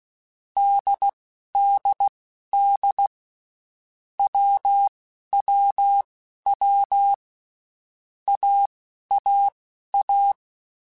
Morse code, heard as a keyed tone, about 16 wpm, spaced as for 11 wpm, 800 Hz. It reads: DDD WWW AAA
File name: Basic Letter Groups By7Groups / AWUNDG / AWUNDG@10